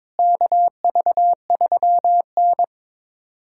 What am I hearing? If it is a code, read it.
Morse code, heard as a keyed tone, about 22 wpm, 700 Hz. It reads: KV3N
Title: USA Callsign DE Callsign - 4 Characters - 22wpm